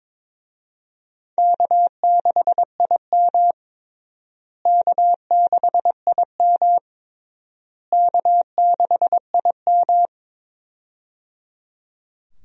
Morse code, heard as a keyed tone, about 21 wpm, 700 Hz. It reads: K6IM K6IM K6IM
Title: USA Call Signs - 4 Characters - Repeated 3x - 22wpm